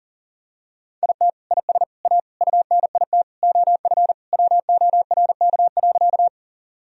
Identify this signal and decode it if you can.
Morse code, heard as a keyed tone, about 40 wpm, 700 Hz. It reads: IT IS A UNIT OF WORK.
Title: Sentences from Top 1000 Words - Review Part 2 - 40wpm